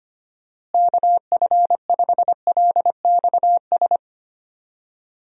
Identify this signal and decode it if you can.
Morse code, heard as a keyed tone, about 25 wpm, 700 Hz. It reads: KF5LXS
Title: USA Call Signs - 6 Characters - Repeated 2x - 25wpm